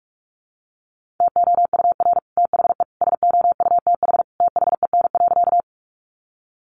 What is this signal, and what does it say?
Morse code, heard as a keyed tone, about 44 wpm, 700 Hz. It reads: TOUR THE SOUTH THEN.
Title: Sentences from Top 1400 Words 45wpm